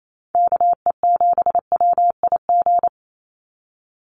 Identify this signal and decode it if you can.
Morse code, heard as a keyed tone, about 28 wpm, 700 Hz. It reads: KE7WIG
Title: USA Call Signs 28wpm